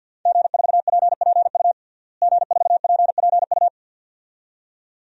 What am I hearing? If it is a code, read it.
Morse code, heard as a keyed tone, about 53 wpm, 700 Hz. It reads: G4PPU G4PPU
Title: DXCC-100 Call Signs - 5 Characters - Repeated 2x - 50wpm